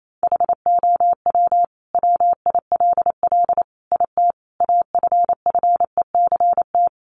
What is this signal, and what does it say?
Morse code, heard as a keyed tone, about 28 wpm, 700 Hz. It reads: HOW WILL IT AFFECT
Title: Sentences from Top 900 Words 28wpm